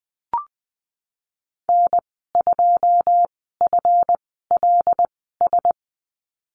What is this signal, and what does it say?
Morse code, heard as a keyed tone, about 20 wpm, spaced as for 15 wpm, 700 Hz. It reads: N2FLS
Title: 147 Lesson 24g - Letter y - Call Signs 20wpm